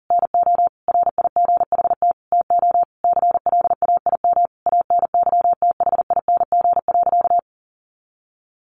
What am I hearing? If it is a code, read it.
Morse code, heard as a keyed tone, about 40 wpm, 700 Hz. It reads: NO RIGHT TO CLAIM ANYTHING.
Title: Sentences from Top 1500 Words 40wpm